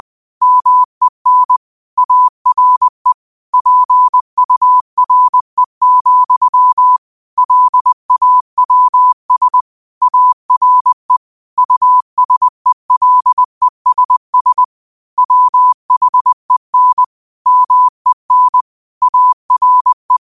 Morse code, heard as a keyed tone, about 20 wpm, 1000 Hz. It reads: MEN ARE PURE, LAWS ARE USELESS WHEN MEN ARE
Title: 20 WPM morse code quotes for Thu, 29 Feb 2024